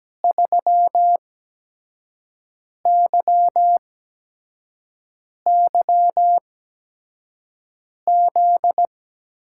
Morse code, heard as a keyed tone, about 17 wpm, spaced as for 5 wpm, 700 Hz. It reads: 3YYZ
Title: USA Call Signs - 6 Characters - 8x Character Spacing - 17wpm